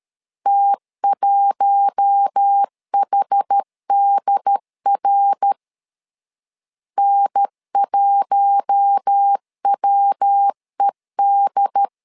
Morse code, heard as a keyed tone, about 13 wpm, 800 Hz. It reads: T1HDR N1WED
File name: LEZIONI / 4 Q1 12wpm 2028